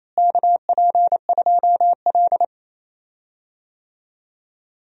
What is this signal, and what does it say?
Morse code, heard as a keyed tone, about 28 wpm, 700 Hz. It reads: KP2L
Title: DXCC-100 Call Signs - 4 Characters - Repeated 2x - 28wpm